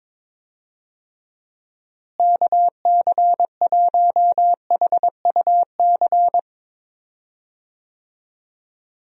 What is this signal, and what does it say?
Morse code, heard as a keyed tone, about 22 wpm, 700 Hz. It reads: KC1HUC
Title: USA Call Signs - 6 Characters - Repeated 2x - 22wpm